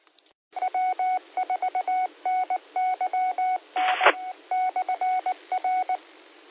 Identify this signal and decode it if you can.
Morse code, heard as a keyed tone, about 19 wpm, 740 Hz. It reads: W4NYN /R